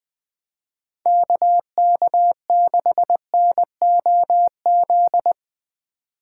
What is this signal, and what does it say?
Morse code, heard as a keyed tone, about 23 wpm, 700 Hz. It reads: KK6NOZ
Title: Speed Racing - USA 2x3 Call Signs 20wpm